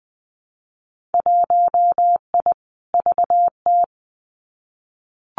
Morse code, heard as a keyed tone, about 20 wpm, 700 Hz. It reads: 1I VT
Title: Winter Field Day Exchanges Encore 20wpm